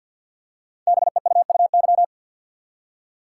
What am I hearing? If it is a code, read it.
Morse code, heard as a keyed tone, about 48 wpm, 700 Hz. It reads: BEUUY
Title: Code Groups - 5 Letters - 50wpm